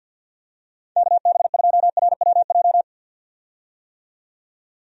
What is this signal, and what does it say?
Morse code, heard as a keyed tone, about 53 wpm, 700 Hz. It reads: KB2RWJ
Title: USA Call Signs - 6 Characters - 50wpm